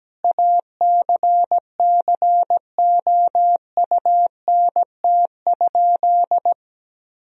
Morse code, heard as a keyed tone, about 17 wpm, 700 Hz. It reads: ACCOUNT?
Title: Sentences from Top 1000 Words - Review Part 4 - 17wpm